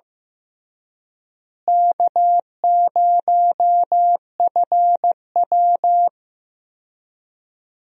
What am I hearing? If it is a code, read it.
Morse code, heard as a keyed tone, about 15 wpm, 700 Hz. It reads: K0FW
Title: USA Call Signs - 4 Characters - Repeated 2x - 15wpm